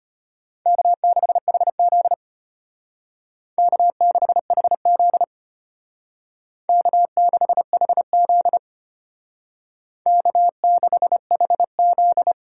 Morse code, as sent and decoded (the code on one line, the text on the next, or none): -.- -.... .... --.. / -.- -.... .... --.. / -.- -.... .... --.. / -.- -.... .... --..
K6HZ K6HZ K6HZ K6HZ